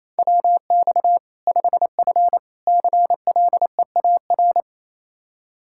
Morse code, heard as a keyed tone, about 27 wpm, 700 Hz. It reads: WX 5F CLEAR
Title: Practice QSO - Exchange 2 - Info - 28wpm